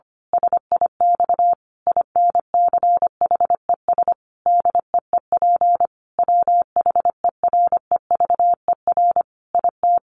Morse code, heard as a keyed tone, about 25 wpm, 700 Hz. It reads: SIX INCHES DEEP WHEREVER IT
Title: A Sound Of Thunder 25wpm